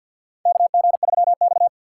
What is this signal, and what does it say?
Morse code, heard as a keyed tone, about 53 wpm, 700 Hz. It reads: KG3X
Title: USA Call Signs - 4 Characters - 2x Character Spacing - 50wpm